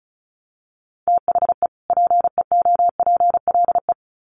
Morse code, heard as a keyed tone, about 34 wpm, 700 Hz. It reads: THE PEOPLE
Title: Sets of 2 Words 35wpm